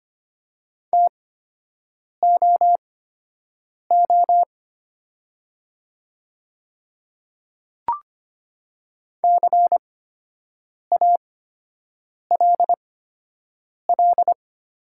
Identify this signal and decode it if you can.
Morse code, heard as a keyed tone, about 25 wpm, spaced as for 7 wpm, 700 Hz. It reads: TOO CALL
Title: 8x Character Spacing - Top 200 Words - 25wpm